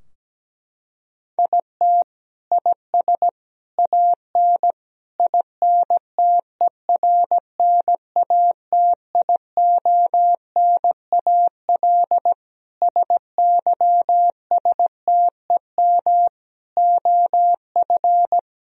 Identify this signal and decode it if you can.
Morse code, heard as a keyed tone, about 17 wpm, 700 Hz. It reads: IT IS AN INTERNATIONAL SYSTEM OF
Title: Sentences from Top 1000 Words - Review Part 4 - 17wpm